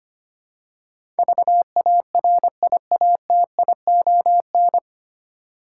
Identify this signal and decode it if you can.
Morse code, heard as a keyed tone, about 25 wpm, 700 Hz. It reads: VARIATION
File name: Nine Letter Words - 25wpm